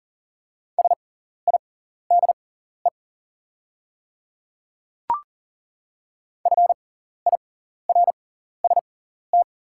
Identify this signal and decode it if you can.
Morse code, heard as a keyed tone, about 40 wpm, spaced as for 14 wpm, 690 Hz. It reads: SIDE FIRST
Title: 6x Character Spacing - Top 200 Words - 40wpm